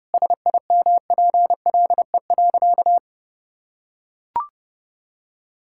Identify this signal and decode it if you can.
Morse code, heard as a keyed tone, about 30 wpm, 700 Hz. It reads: SIMPLE.
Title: Sentences from Top 1000 Words - Review Part 6 - 30wpm